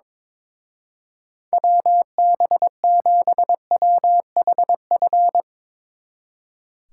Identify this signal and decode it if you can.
Morse code, heard as a keyed tone, about 23 wpm, 700 Hz. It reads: WB7WHF
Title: Speed Racing - USA 2x3 Call Signs 22wpm